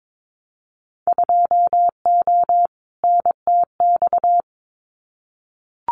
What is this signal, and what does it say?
Morse code, heard as a keyed tone, about 22 wpm, 700 Hz. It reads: 2O NTX
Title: Winter Field Day Exchanges 22wpm